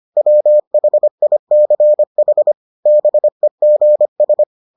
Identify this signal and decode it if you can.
Morse code, heard as a keyed tone, about 25 wpm, 600 Hz. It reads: WHICH BEGS